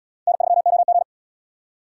Morse code, heard as a keyed tone, about 70 wpm, 690 Hz. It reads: N3OG